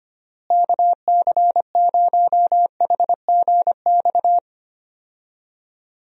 Morse code, heard as a keyed tone, about 25 wpm, 700 Hz. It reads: KC0HGX
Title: USA Call Signs - 6 Characters - Repeated 3x - 25wpm